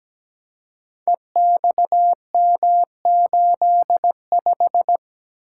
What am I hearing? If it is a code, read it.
Morse code, heard as a keyed tone, about 17 wpm, 700 Hz. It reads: EXM85
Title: Code Groups - 5 Letter:Numbers - 17wpm